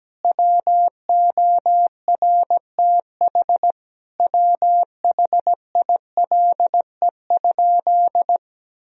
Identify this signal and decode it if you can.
Morse code, heard as a keyed tone, about 17 wpm, 700 Hz. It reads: WORTH WHILE?
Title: Sentences from Top 1000 Words - Review Part 1 - 17wpm